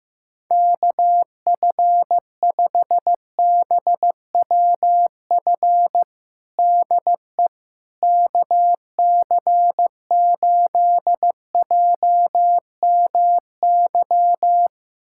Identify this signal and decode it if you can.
Morse code, heard as a keyed tone, about 15 wpm, 700 Hz. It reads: KF5BWF DE KC8JMY
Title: USA Callsign DE Callsign - 6 Characters - 15wpm